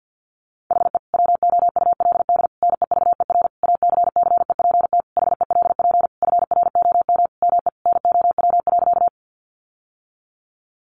Morse code, heard as a keyed tone, about 48 wpm, 700 Hz. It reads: HE WOULD NEVER ACCEPT HELP FROM ME NOW.
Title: Sentences from Top 1300 Words 50wpm